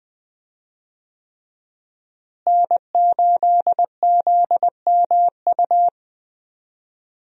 Morse code, heard as a keyed tone, about 20 wpm, 700 Hz. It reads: N8ZMU
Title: USA Call Signs - 5 Characters - Repeated 3x - 20wpm